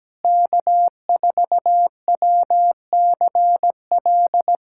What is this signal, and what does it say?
Morse code, heard as a keyed tone, about 17 wpm, 700 Hz. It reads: K4WCL
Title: K1USN SST QSO Practice - 17wpm